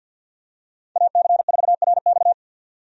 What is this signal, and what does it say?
Morse code, heard as a keyed tone, about 52 wpm, 700 Hz. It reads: AC4RX
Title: USA Call Signs - 5 Characters - 50wpm